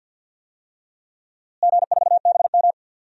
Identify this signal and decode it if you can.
Morse code, heard as a keyed tone, about 52 wpm, 700 Hz. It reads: G4BM
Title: DXCC-100 Call Signs - 4 Characters - Repeated 2x - 50wpm